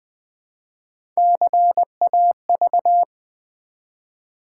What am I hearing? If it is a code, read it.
Morse code, heard as a keyed tone, about 20 wpm, 700 Hz. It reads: CAV
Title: Build A Word - 1 Letter At A Time - 3 to 5-Letter Words - 20wpm